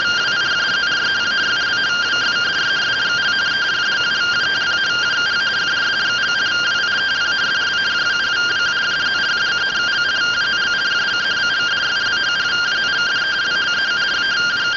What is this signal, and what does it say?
Two-tone teleprinter signal, about 170 Hz shift.